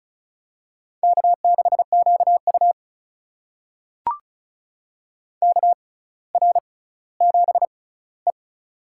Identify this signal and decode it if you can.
Morse code, heard as a keyed tone, about 36 wpm, spaced as for 18 wpm, 700 Hz. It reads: K6QU KR7E
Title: USA Call Signs - 4 Characters - 6x Character Spacing - 35wpm